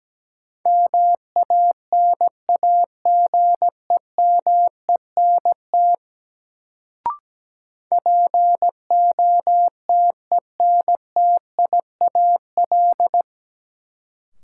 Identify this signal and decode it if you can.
Morse code, heard as a keyed tone, about 17 wpm, 700 Hz. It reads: MANAGEMENT POTENTIAL
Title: Sentences from Top 1000 Words - Review Part 4 - New Word Drill - 17wpm